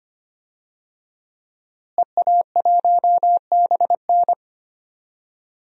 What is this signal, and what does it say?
Morse code, heard as a keyed tone, about 25 wpm, 700 Hz. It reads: EA1BN
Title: DXCC-100 Call Signs - 5 Characters - Repeated 2x - 25wpm